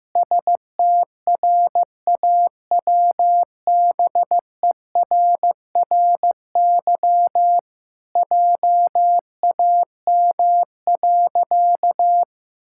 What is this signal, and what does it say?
Morse code, heard as a keyed tone, about 15 wpm, 700 Hz. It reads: STRAWBERRY JAM.
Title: Pangram Challenge v2 - 15wpm